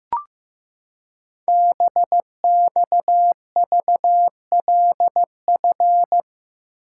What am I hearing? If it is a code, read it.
Morse code, heard as a keyed tone, about 15 wpm, 700 Hz. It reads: BXVLF
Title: Code Groups - 5 Letters - 15wpm